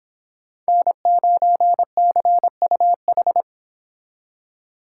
Morse code, heard as a keyed tone, about 30 wpm, 700 Hz. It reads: N9CUH